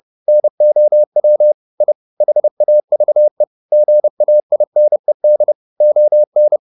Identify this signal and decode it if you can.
Morse code, heard as a keyed tone, about 30 wpm, 600 Hz. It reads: NOW I HAVE GAINED ON